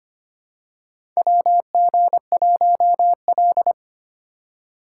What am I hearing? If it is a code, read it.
Morse code, heard as a keyed tone, about 25 wpm, 700 Hz. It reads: WG1L
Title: USA Call Signs - 4 Characters - 4x Character Spacing - 25wpm